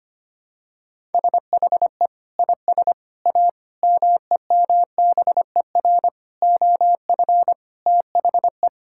Morse code, heard as a keyed tone, about 25 wpm, 700 Hz. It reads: SHE IS A MEMBER OF THE
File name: practice / 25wpm / Sentences with 3-Word Phrases - 25wpm